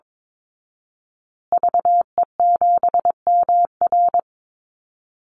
Morse code, heard as a keyed tone, about 22 wpm, 690 Hz. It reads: VE7MR
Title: Call Signs Repeated 2x - 22wpm